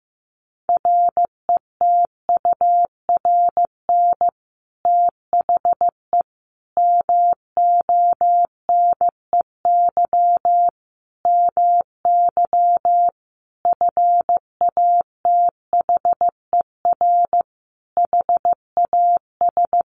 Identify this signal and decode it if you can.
Morse code, heard as a keyed tone, about 15 wpm, 700 Hz. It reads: RETURN THE MONEY MY FATHER HAS
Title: Sentences from Top 1200 Words 15wpm